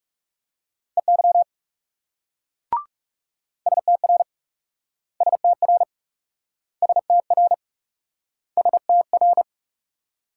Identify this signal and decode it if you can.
Morse code, heard as a keyed tone, about 37 wpm, 700 Hz. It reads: EY STR STR STR STR
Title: Speed-Racing Multi-letter Phonemes - 30wpm